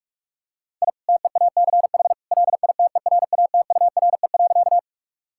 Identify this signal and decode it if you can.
Morse code, heard as a keyed tone, about 46 wpm, 700 Hz. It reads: I TEACH LITERATURE.
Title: Sentences from Top 1000 Words - Review Part 6 - 45wpm